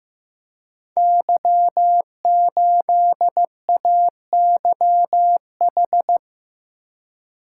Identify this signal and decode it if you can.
Morse code, heard as a keyed tone, about 15 wpm, 700 Hz. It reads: Y8AYH
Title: Code Groups - 5 Letter:Numbers - 15wpm